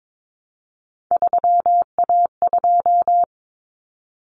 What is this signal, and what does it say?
Morse code, heard as a keyed tone, about 22 wpm, 700 Hz. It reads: 3A2
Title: Common Cut Numbers - 3-Digit 22wpm